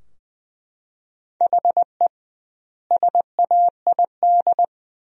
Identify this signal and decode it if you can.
Morse code, heard as a keyed tone, about 21 wpm, 700 Hz. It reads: HE SAID
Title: 2x Word-Spacing - Sentences from Top 100 Words - 20wpm